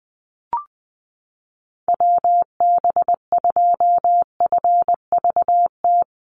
Morse code, heard as a keyed tone, about 20 wpm, 700 Hz. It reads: WB2FVT
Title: USA Call Signs Repeated 3x 20wpm